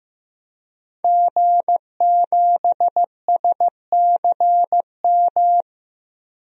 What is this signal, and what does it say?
Morse code, heard as a keyed tone, about 15 wpm, 700 Hz. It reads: G7SCM
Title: DXCC-100 Call Signs - 5 Characters - Repeated 2x - 15wpm